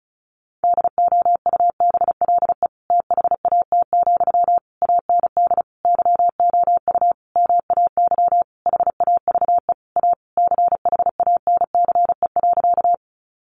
Morse code, heard as a keyed tone, about 34 wpm, 700 Hz. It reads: DOUBLE THAT, AND YOU MAY HAVE A CHANCE.
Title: Sentences from Top 1400 Words 35wpm